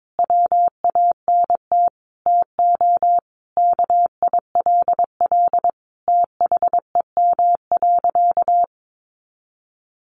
Morse code, heard as a keyed tone, about 22 wpm, 700 Hz. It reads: WANT TO KILL THEM.
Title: Easy Phrases 22wpm